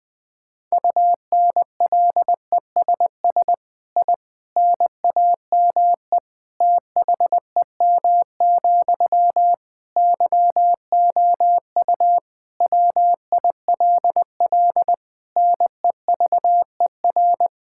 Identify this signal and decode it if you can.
Morse code, heard as a keyed tone, about 21 wpm, 700 Hz. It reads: UNLESS I NAME THEM, YOU WILL NEVER
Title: Sentences from Top 1000 Words - Review Part 6 - 20wpm